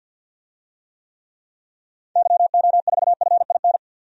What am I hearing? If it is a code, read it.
Morse code, heard as a keyed tone, about 50 wpm, 700 Hz. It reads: YO4FIN
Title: DXCC-100 Call Signs - 6 Characters - 50wpm